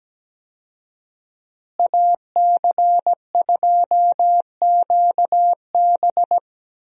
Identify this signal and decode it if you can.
Morse code, heard as a keyed tone, about 17 wpm, 700 Hz. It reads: AC2QB
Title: USA Call Signs - 5 Characters - Repeated 2x - 17wpm